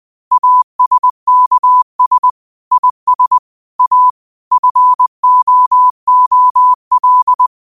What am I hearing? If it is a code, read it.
Morse code, heard as a keyed tone, about 20 wpm, 1000 Hz. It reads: ASKS IS A FOOL
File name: qotdfiles / 20wpm